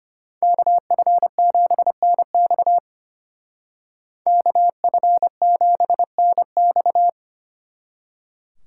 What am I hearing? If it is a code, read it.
Morse code, heard as a keyed tone, about 33 wpm, 700 Hz. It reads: KF7NX KF7NX